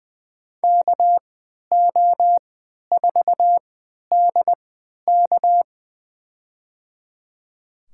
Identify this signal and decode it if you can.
Morse code, heard as a keyed tone, about 20 wpm, spaced as for 11 wpm, 700 Hz. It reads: KO4DK
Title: USA Call Signs - 5 Characters - 3x Character Spacing - 20wpm